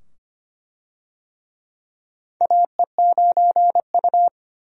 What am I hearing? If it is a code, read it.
Morse code, heard as a keyed tone, about 25 wpm, 700 Hz. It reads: AE9U
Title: USA Call Signs - 4 Characters - 25wpm